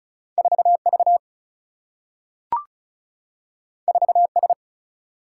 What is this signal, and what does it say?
Morse code, heard as a keyed tone, about 36 wpm, 700 Hz. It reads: 4 V 4 S
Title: Single-Letter Single-Number 35wpm